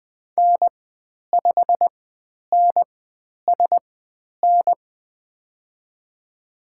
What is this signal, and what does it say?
Morse code, heard as a keyed tone, about 20 wpm, spaced as for 10 wpm, 700 Hz. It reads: N5NSN
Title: 109 Lesson 18g - Letter m - Call Signs 20wpm